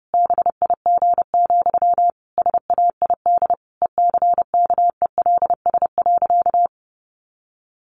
Morse code, heard as a keyed tone, about 30 wpm, 700 Hz. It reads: BIG, SAID ECKELS.
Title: A Sound Of Thunder 30wpm